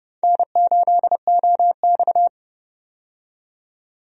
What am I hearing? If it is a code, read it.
Morse code, heard as a keyed tone, about 32 wpm, 700 Hz. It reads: N8OX
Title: USA Call Signs - 4 Characters - 30wpm